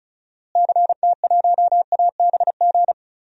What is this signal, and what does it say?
Morse code, heard as a keyed tone, about 34 wpm, 700 Hz. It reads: CT1ABG
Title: DXCC-100 Call Signs - 6 Characters - Repeated 2x - 35wpm